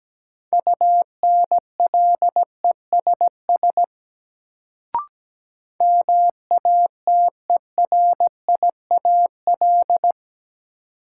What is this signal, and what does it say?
Morse code, heard as a keyed tone, about 17 wpm, 700 Hz. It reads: UNLESS MATERIAL
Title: Sentences from Top 1000 Words - Review Part 5 - New Word Drill - 17wpm